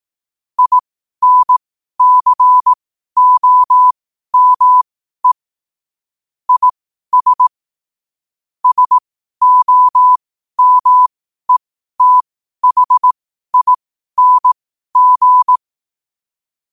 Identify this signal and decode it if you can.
Morse code, heard as a keyed tone, about 18 wpm, spaced as for 12 wpm, 1000 Hz. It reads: INCOME IS SOMETHING
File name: qotdfiles / 12wpm